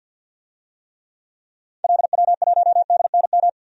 Morse code, heard as a keyed tone, about 49 wpm, 700 Hz. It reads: LW1DNM
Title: DXCC-100 Call Signs - 6 Characters - 50wpm